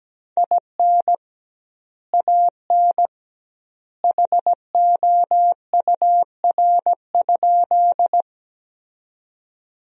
Morse code, heard as a keyed tone, about 17 wpm, 700 Hz. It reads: IN AN HOUR?
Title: 2x Word-Spacing - Sentences from Top 400 Words - 17wpm